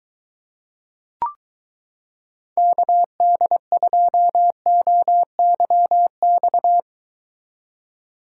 Morse code, heard as a keyed tone, about 23 wpm, 700 Hz. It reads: KD2OYX